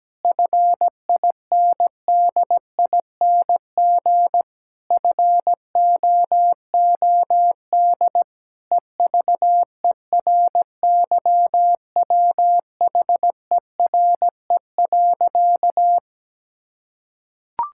Morse code, heard as a keyed tone, about 17 wpm, 700 Hz. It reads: FINDING FOOD EVERYWHERE.
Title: Sentences from Top 1600 Words 17wpm